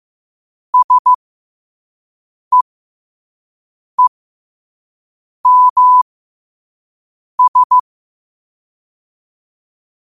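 Morse code, heard as a keyed tone, about 15 wpm, spaced as for 5 wpm, 1000 Hz. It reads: SEEMS
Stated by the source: Morse code practice at 5 wpm.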